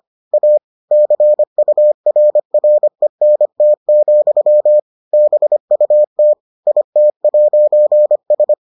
Morse code, heard as a keyed tone, about 25 wpm, 600 Hz. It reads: A CURRENT, BUT IT'S